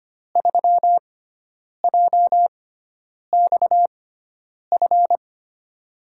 Morse code, heard as a keyed tone, about 25 wpm, spaced as for 9 wpm, 700 Hz. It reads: 3JXF
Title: USA Call Signs - 6 Characters - 6x Character Spacing - 25wpm